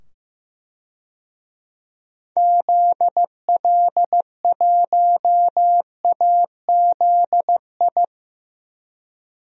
Morse code, heard as a keyed tone, about 15 wpm, 700 Hz. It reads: ZL1AZI
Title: DXCC-100 Call Signs - 6 Characters - 15wpm